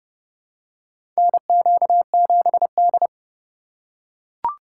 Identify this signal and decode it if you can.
Morse code, heard as a keyed tone, about 31 wpm, 700 Hz. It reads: NQ7D
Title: USA Call Signs - 4 Characters - 8x Character Spacing - 30wpm